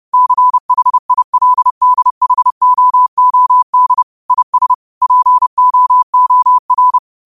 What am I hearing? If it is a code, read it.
Morse code, heard as a keyed tone, about 30 wpm, 1000 Hz. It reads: CHILDHOOD IS POOR
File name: qotdfiles / 30wpm